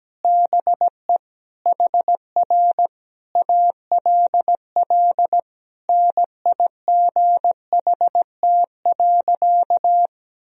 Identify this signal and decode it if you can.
Morse code, heard as a keyed tone, about 17 wpm, 700 Hz. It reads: BE HR ALL NIGHT.
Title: Sentences from Top 300 Words and Abbreviations 17wpm